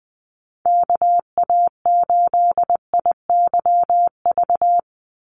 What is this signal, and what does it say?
Morse code, heard as a keyed tone, about 20 wpm, 700 Hz. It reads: KA8IYV
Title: USA Call Signs 20wpm